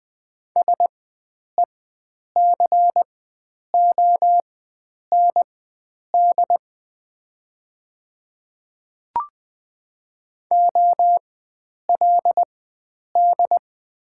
Morse code, heard as a keyed tone, about 20 wpm, spaced as for 9 wpm, 700 Hz. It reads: SECOND OLD
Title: 4x Character Spacing - Top 500 Words - 20wpm